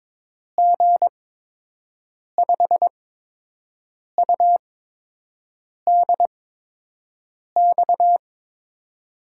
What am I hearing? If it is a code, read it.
Morse code, heard as a keyed tone, about 22 wpm, spaced as for 6 wpm, 700 Hz. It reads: G5UDX
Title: USA Call Signs - 6 Characters - 8x Character Spacing - 22wpm